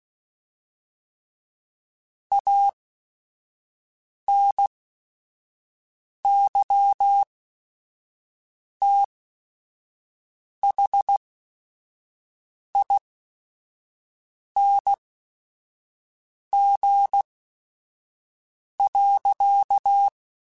Morse code, heard as a keyed tone, about 16 wpm, spaced as for 5 wpm, 800 Hz. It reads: ANYTHING.
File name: Quotes / Quotes-mp3s / StandFor @5